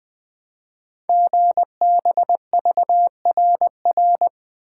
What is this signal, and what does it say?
Morse code, heard as a keyed tone, about 20 wpm, 700 Hz. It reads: GBVRR
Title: Code Groups - 5 Letters - 20wpm